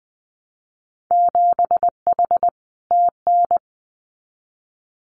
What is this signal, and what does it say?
Morse code, heard as a keyed tone, about 20 wpm, 700 Hz. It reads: 7H TN
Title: Winter Field Day Exchanges Encore 20wpm